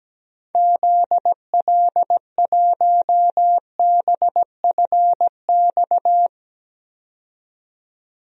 Morse code, heard as a keyed tone, about 17 wpm, 700 Hz. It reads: ZL1BFX
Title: DXCC-100 Call Signs - 6 Characters - 17wpm